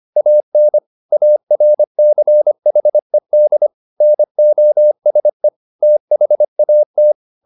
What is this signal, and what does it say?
Morse code, heard as a keyed tone, about 25 wpm, 600 Hz. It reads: AN ARCHED NOSE THAT